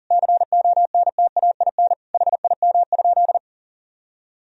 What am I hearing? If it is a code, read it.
Morse code, heard as a keyed tone, about 40 wpm, 700 Hz. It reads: CONTAIN HIM?
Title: Sentences from Top 1000 Words - Review Part 2 - 40wpm